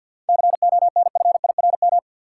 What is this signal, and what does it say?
Morse code, heard as a keyed tone, about 50 wpm, 700 Hz. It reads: CONFIRM
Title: Seven Letter Words - 50wpm